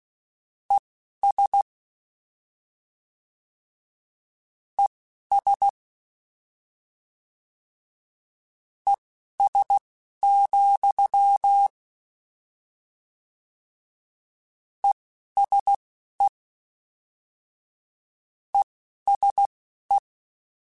Morse code, heard as a keyed tone, about 16 wpm, spaced as for 12 wpm, 800 Hz. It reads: ES ES ES, ESE ESE
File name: General / General-mp3s / General @10